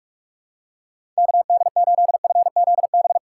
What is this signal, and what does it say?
Morse code, heard as a keyed tone, about 42 wpm, 700 Hz. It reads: KD8FZB
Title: USA Call Signs - 6 Characters - 6x Character Spacing - 45wpm